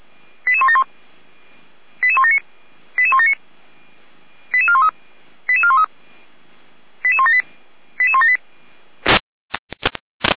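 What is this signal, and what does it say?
Five-tone selective call (ZVEI), heard as quick runs of five tones, tones of about 80 ms. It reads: 88171 88179 88179 88413 88413 88178 88178
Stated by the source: ZVEI five-tone selective call 88352, received with an SDR.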